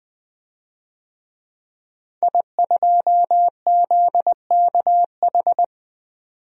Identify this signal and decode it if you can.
Morse code, heard as a keyed tone, about 20 wpm, 700 Hz. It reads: I2ZKH
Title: DXCC-100 Call Signs - 5 Characters - 20wpm